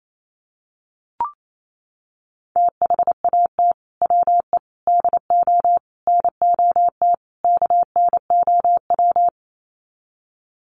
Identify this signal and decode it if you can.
Morse code, heard as a keyed tone, about 28 wpm, 700 Hz. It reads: THAT WE DO NOT KNOW
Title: Sets of 5 Words Encore2 28wpm